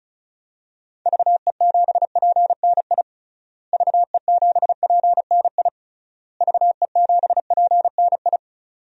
Morse code, heard as a keyed tone, about 34 wpm, 700 Hz. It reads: VE7PNI VE7PNI VE7PNI
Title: DXCC-100 Call Signs - 6 Characters - Repeated 3x - 35wpm